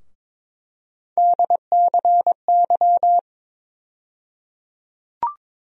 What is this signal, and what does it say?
Morse code, heard as a keyed tone, about 22 wpm, 700 Hz. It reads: DCY
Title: USA Call Sign Postfixes - 3 Letters 22wpm